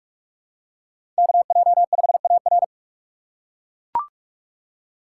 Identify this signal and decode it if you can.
Morse code, heard as a keyed tone, about 44 wpm, 700 Hz. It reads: KJ5AR
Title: USA Call Signs - 5 Characters - 6x Character Spacing - 45wpm